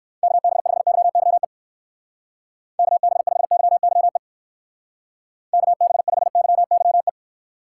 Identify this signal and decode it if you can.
Morse code, heard as a keyed tone, about 58 wpm, 700 Hz. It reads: KB5YYE KB5YYE KB5YYE